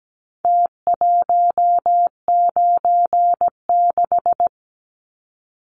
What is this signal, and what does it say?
Morse code, heard as a keyed tone, about 17 wpm, 700 Hz. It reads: T196
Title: Numbers - Contest Style - 4-Digit 17wpm